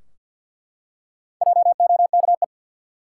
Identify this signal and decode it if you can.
Morse code, heard as a keyed tone, about 49 wpm, 700 Hz. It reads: JOKE
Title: Four Letter Words - 50wpm